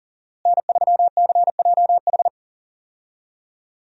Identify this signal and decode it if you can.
Morse code, heard as a keyed tone, about 40 wpm, 700 Hz. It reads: N3CJH
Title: USA Call Signs - 5 Characters - Repeated 2x - 40wpm